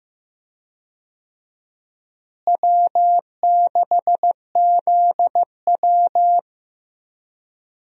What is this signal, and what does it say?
Morse code, heard as a keyed tone, about 15 wpm, 700 Hz. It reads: W6ZW
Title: USA Call Signs - 4 Characters - Repeated 2x - 15wpm